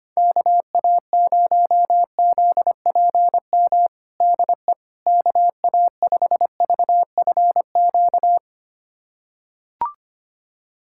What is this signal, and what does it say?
Morse code, heard as a keyed tone, about 25 wpm, 700 Hz. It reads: KA0ZPM DE KA5VFQ
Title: USA Callsign DE Callsign - 6 Characters - 25wpm